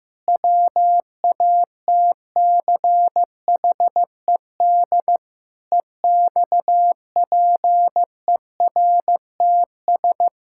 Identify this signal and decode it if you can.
Morse code, heard as a keyed tone, about 15 wpm, 700 Hz. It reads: WATCHED EXPERTS
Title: Pangram Challenge v2 - 15wpm